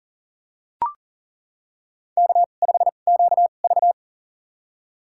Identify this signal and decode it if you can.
Morse code, heard as a keyed tone, about 40 wpm, spaced as for 29 wpm, 700 Hz. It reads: K5QV
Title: USA Call Signs - 4 Characters - 2x Character Spacing - 40wpm